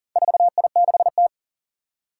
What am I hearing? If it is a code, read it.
Morse code, heard as a keyed tone, about 39 wpm, 700 Hz. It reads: 4I6T
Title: Four Challenging Letter-Number Combinations 40wpm